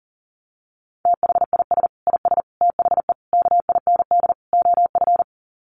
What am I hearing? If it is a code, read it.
Morse code, heard as a keyed tone, about 37 wpm, 700 Hz. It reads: THIS IS THE KIND OF
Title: Sets of 5 Words Encore 40wpm